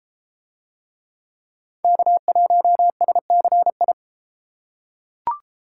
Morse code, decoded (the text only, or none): K1SCI